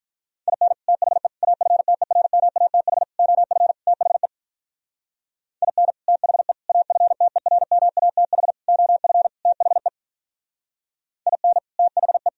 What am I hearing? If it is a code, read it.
Morse code, heard as a keyed tone, about 43 wpm, 700 Hz. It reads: IN THE AFTERMATH OF THE IN THE AFTERMATH OF THE IN THE
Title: Speed-Racing - Sets of 5 Words 35wpm